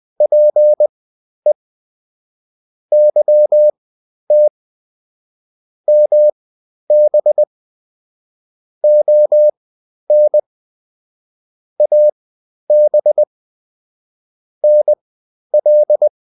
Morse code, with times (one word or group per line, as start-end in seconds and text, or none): 0.2-1.5 PE
2.9-4.5 YT
5.9-7.4 MB
8.8-10.4 ON
11.8-13.2 AB
14.6-16.1 NL